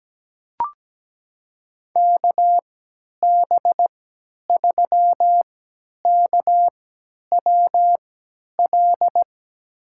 Morse code, heard as a keyed tone, about 17 wpm, spaced as for 10 wpm, 700 Hz. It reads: KB3KWL
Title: USA Call Signs - 6 Characters - 3x Character Spacing - 17wpm